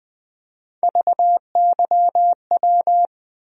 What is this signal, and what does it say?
Morse code, heard as a keyed tone, about 20 wpm, 700 Hz. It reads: VYW
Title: USA Call Sign Postfixes - 3 Letters 20wpm